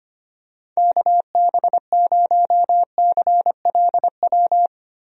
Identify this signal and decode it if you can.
Morse code, heard as a keyed tone, about 25 wpm, 700 Hz. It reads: KB0CLW
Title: USA Call Signs - 6 Characters - 25wpm